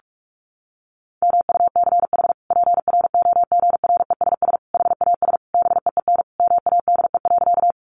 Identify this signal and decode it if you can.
Morse code, heard as a keyed tone, about 44 wpm, 700 Hz. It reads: MUCH PROGRESS HAS BEEN MADE.
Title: Sentences from Top 1400 Words 45wpm